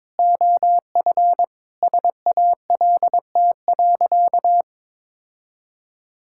Morse code, heard as a keyed tone, about 22 wpm, 700 Hz. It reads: OF SALT.
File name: Sentences with 7-Word Phrases - 22wpm